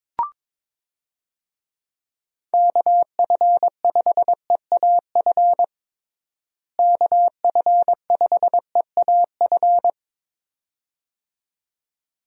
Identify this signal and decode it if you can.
Morse code, heard as a keyed tone, about 22 wpm, 700 Hz. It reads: KF5EAF KF5EAF
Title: USA Call Signs - 6 Characters - Repeated 2x - 22wpm